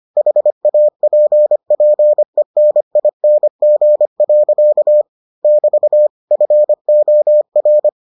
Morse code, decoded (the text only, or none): HAPPENING. <BT> FOR